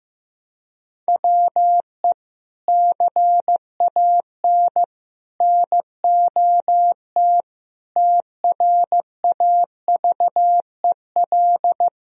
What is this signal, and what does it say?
Morse code, heard as a keyed tone, about 15 wpm, 700 Hz. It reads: WE CAN NOT TRAVEL
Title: Sentences from Top 1000 Words - Review Part 2 - 15wpm